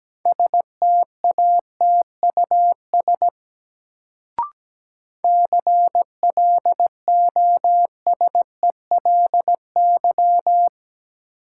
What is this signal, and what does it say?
Morse code, heard as a keyed tone, about 17 wpm, 700 Hz. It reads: STATUS CLOSELY
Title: Sentences from Top 1400 Words - New Word Drill - 17wpm